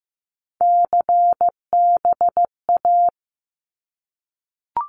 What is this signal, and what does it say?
Morse code, heard as a keyed tone, about 15 wpm, 700 Hz. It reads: CBA
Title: Abbreviations 15wpm